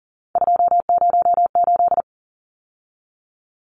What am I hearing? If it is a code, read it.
Morse code, heard as a keyed tone, about 40 wpm, 700 Hz. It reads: 208
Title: Numbers - 3-Digit 40wpm